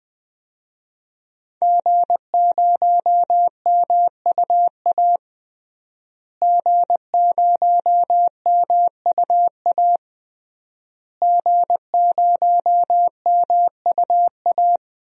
Morse code, heard as a keyed tone, about 20 wpm, 700 Hz. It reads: G0MUA G0MUA G0MUA
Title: DXCC-100 Call Signs - 5 Characters - Repeated 3x - 20wpm